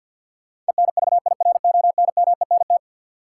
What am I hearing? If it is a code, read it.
Morse code, heard as a keyed tone, about 48 wpm, 700 Hz. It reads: ENVIRONMENT
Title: Sentences from Top 1000 Words - Review Part 5 - New Word Drill - 50wpm